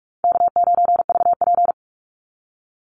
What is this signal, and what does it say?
Morse code, heard as a keyed tone, about 44 wpm, 700 Hz. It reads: K9VP
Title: Easy Call Signs Repeated 2x - 45wpm